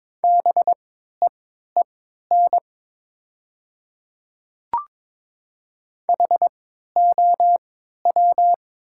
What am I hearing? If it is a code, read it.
Morse code, heard as a keyed tone, about 22 wpm, spaced as for 12 wpm, 700 Hz. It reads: BEEN HOW
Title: 3x Character Spacing - Top 100 Words - 22wpm